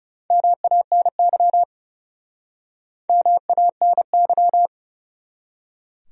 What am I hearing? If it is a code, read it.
Morse code, heard as a keyed tone, about 42 wpm, 700 Hz. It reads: MANY MANY